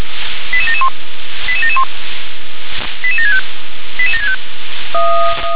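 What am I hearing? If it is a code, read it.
Five-tone selective call (ZVEI), heard as quick runs of five tones, tones of about 70 ms. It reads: 88771 88771 88765 88765
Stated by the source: ZVEI five-tone selective call received by SDR.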